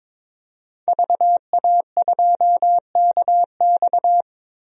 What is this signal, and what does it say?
Morse code, heard as a keyed tone, about 21 wpm, 700 Hz. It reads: VA2KX
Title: DXCC-100 Call Signs - 5 Characters - 22wpm